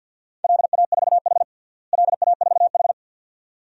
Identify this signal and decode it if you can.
Morse code, heard as a keyed tone, about 52 wpm, 700 Hz. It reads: LA4H LA4H
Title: DXCC-100 Call Signs - 4 Characters - Repeated 2x - 50wpm